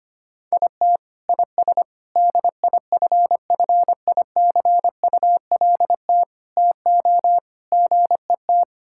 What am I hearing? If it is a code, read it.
Morse code, heard as a keyed tone, about 25 wpm, 700 Hz. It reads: IT IS DIFFICULT TO GET
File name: practice / 25wpm / Sentences with 9-Word Phrases - 25wpm